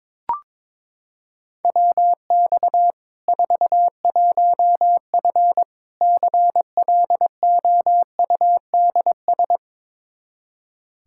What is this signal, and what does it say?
Morse code, heard as a keyed tone, about 21 wpm, 700 Hz. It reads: WX 41F CLOUDS
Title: Practice QSO - Exchange 2 - Info - 22wpm